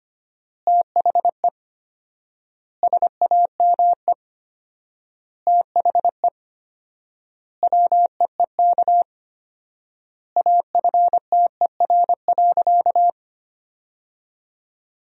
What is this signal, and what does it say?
Morse code, heard as a keyed tone, about 25 wpm, 700 Hz. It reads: THE SAME THE WEEK AFTER.
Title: 4x Word-Spacing - Sentences from Top 500 Words - 25wpm